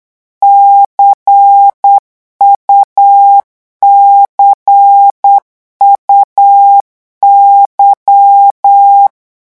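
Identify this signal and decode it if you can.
Morse code, heard as a keyed tone, about 8 wpm, 800 Hz. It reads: CUCUY
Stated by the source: Morse code revision lesson at 8 words per minute, from a French amateur radio course.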